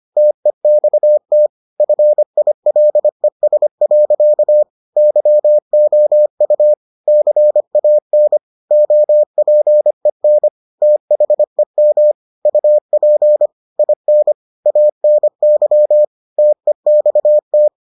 Morse code, heard as a keyed tone, about 25 wpm, 600 Hz. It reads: TEXT FILES. YOU CAN OPEN THEM UP IN ANY TEXT